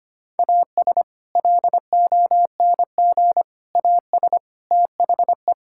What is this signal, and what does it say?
Morse code, heard as a keyed tone, about 25 wpm, 700 Hz. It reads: AS LONG AS THE
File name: practice / 25wpm / Sentences with 4-Word Phrases - 25wpm